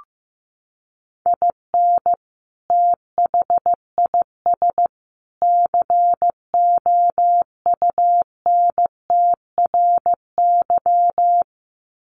Morse code, heard as a keyed tone, about 15 wpm, 700 Hz. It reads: IN THIS COUNTRY
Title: Sets of 3 Words 15wpm